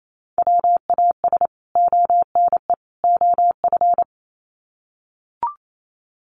Morse code, heard as a keyed tone, about 28 wpm, 700 Hz. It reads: WAS ONE OF
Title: Sets of 3 Words 28wpm